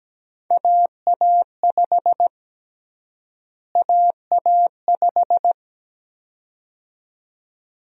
Morse code, heard as a keyed tone, about 17 wpm, 700 Hz. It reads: AA5 AA5
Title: USA Call Sign Prefixes - Repeated 2x - 17wpm